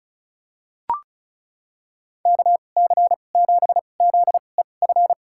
Morse code, heard as a keyed tone, about 36 wpm, spaced as for 25 wpm, 700 Hz. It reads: KC7ZEF
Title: USA Call Signs - 6 Characters - 2x Character Spacing - 35wpm